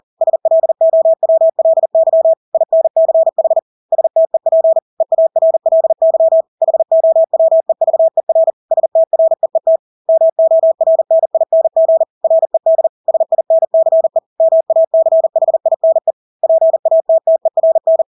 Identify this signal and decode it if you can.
Morse code, heard as a keyed tone, about 40 wpm, 650 Hz. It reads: SLOWLY INCH STEP EARLY HOWEVER STREET MORNING RED SINCE MACHINE PATTERN
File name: Engelske ord 40-40 wpm